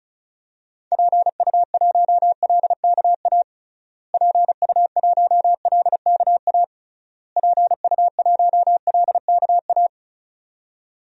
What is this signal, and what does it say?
Morse code, heard as a keyed tone, about 34 wpm, 700 Hz. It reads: PU1LKA PU1LKA PU1LKA
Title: DXCC-100 Call Signs - 6 Characters - Repeated 3x - 35wpm